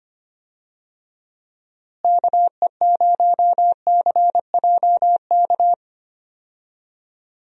Morse code, heard as a keyed tone, about 25 wpm, 700 Hz. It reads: KE0CJK
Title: USA Call Signs - 6 Characters - Repeated 2x - 25wpm